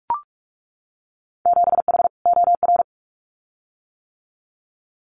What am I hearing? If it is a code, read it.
Morse code, heard as a keyed tone, about 44 wpm, 700 Hz. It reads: 7H OR
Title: Winter Field Day Exchanges Encore 45wpm